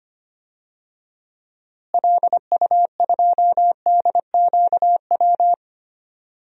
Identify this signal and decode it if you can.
Morse code, heard as a keyed tone, about 25 wpm, 700 Hz. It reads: LU2DQW
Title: DXCC-100 Call Signs - 6 Characters - Repeated 3x - 25wpm